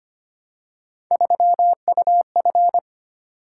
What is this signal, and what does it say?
Morse code, heard as a keyed tone, about 25 wpm, 700 Hz. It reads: 3UF
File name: Three Letter-Number Combinations - 25wpm